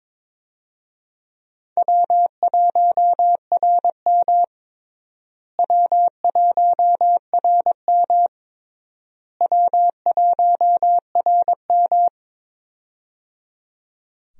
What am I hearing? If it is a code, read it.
Morse code, heard as a keyed tone, about 22 wpm, 700 Hz. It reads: W1RM W1RM W1RM
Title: USA Call Signs - 4 Characters - Repeated 3x - 22wpm